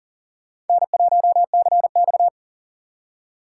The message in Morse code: N1CX

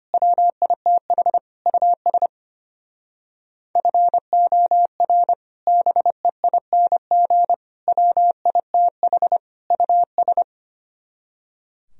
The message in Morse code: WITH US FOR BEING WITH US

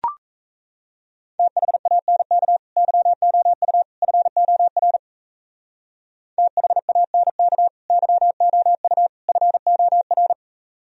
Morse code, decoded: THANK YOU FOR THANK YOU FOR

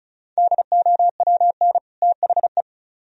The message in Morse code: DOWN THE

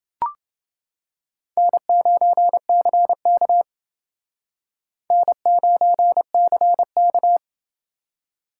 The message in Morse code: N9CK N9CK

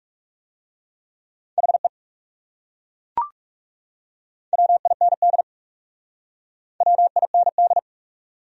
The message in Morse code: HE WIND WIND

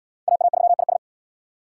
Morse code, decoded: AA3IS

